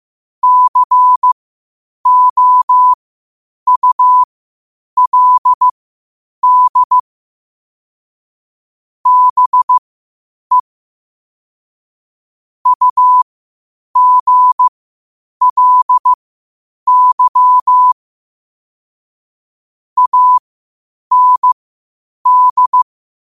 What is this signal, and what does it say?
Morse code, heard as a keyed tone, about 15 wpm, spaced as for 8 wpm, 1000 Hz. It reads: COULD BE UGLY AND